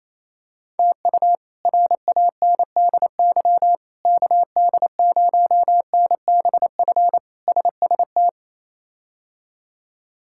Morse code, heard as a keyed tone, about 27 wpm, 700 Hz. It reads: TU RANDY KD0NBF SST
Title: K1USN SST QSO Practice - 28wpm